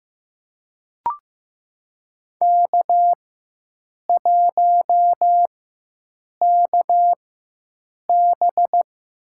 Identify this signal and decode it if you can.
Morse code, heard as a keyed tone, about 15 wpm, spaced as for 7 wpm, 700 Hz. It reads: K1KB
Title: USA Call Signs - 4 Characters - 4x Character Spacing - 15wpm